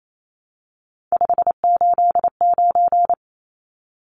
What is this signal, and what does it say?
Morse code, heard as a keyed tone, about 28 wpm, 700 Hz. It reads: 589
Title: Numbers - 3-Digit 28wpm